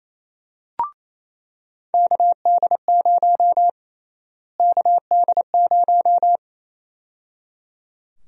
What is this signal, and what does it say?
Morse code, heard as a keyed tone, about 28 wpm, 700 Hz. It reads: KD0 KD0
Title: USA Call Sign Prefixes - Repeated 2x - 28wpm